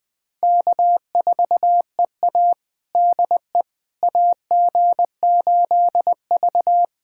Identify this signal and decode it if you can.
Morse code, heard as a keyed tone, about 21 wpm, 700 Hz. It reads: K4EA DE AG8V
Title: USA Callsign DE Callsign - 4 Characters - 20wpm